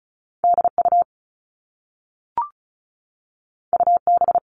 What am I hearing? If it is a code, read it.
Morse code, heard as a keyed tone, about 36 wpm, 700 Hz. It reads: DU UB
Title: All Cut Numbers - 2-Digit 35wpm